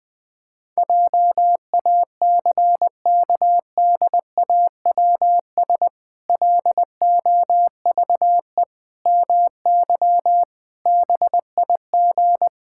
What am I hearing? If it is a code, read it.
Morse code, heard as a keyed tone, about 20 wpm, 700 Hz. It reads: JACKDAWS LOVE MY BIG
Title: Pangram Challenge v2 - 20wpm